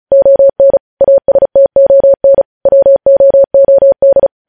Morse code, heard as a keyed tone, about 36 wpm, 550 Hz. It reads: ON ASTON WOOD
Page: News Headlines in Morse Code at 35 WPM